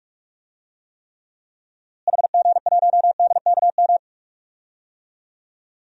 Morse code, heard as a keyed tone, about 46 wpm, 700 Hz. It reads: HG1DKM
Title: DXCC-100 Call Signs - 6 Characters - Repeated 3x - 45wpm